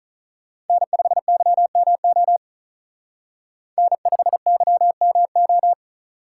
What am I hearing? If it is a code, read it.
Morse code, heard as a keyed tone, about 42 wpm, 700 Hz. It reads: N5YMO N5YMO
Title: Speed Racing - USA 1x3 Call Signs 35wpm